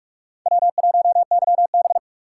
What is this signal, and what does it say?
Morse code, heard as a keyed tone, about 44 wpm, 700 Hz. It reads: W1YB
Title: USA Call Signs - 6x Character Spacing - 45wpm